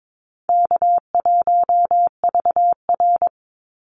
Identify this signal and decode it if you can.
Morse code, heard as a keyed tone, about 22 wpm, 700 Hz. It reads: K1VR
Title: Call Signs Repeated 2x - 22wpm